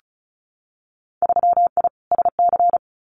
Morse code, heard as a keyed tone, about 36 wpm, 700 Hz. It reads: 3I SC
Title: Winter Field Day Exchanges Encore 35wpm